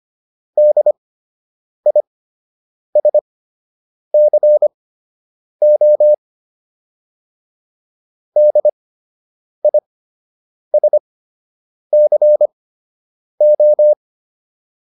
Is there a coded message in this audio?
Yes: Morse code from 0.6 to 13.9 s, heard as a keyed tone, about 25 wpm, spaced as for 8 wpm, 600 Hz. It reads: DISCO DISCO